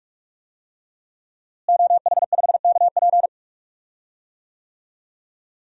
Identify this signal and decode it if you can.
Morse code, heard as a keyed tone, about 46 wpm, 700 Hz. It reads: OH5KP
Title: DXCC-100 Call Signs - 5 Characters - Repeated 2x - 45wpm